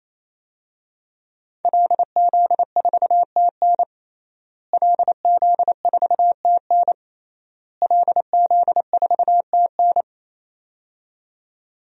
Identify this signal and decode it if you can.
Morse code, heard as a keyed tone, about 28 wpm, 700 Hz. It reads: LZ4TN LZ4TN LZ4TN
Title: DXCC-100 Call Signs - 5 Characters - Repeated 3x - 28wpm